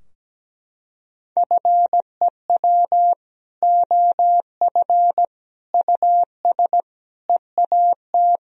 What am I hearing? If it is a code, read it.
Morse code, heard as a keyed tone, about 17 wpm, 700 Hz. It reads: FEW OF US EAT
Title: Sentences from Top 1000 Words - Review Part 3 - 17wpm